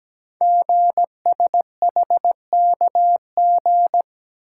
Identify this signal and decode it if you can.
Morse code, heard as a keyed tone, about 17 wpm, 700 Hz. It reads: GSHKG
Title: Code Groups - 5 Letters - 17wpm